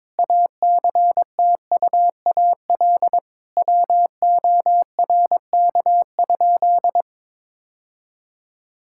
Morse code, heard as a keyed tone, about 22 wpm, 700 Hz. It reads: ACTUAL WORK?
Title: Sentences from Top 1000 Words - Review Part 5 - 22wpm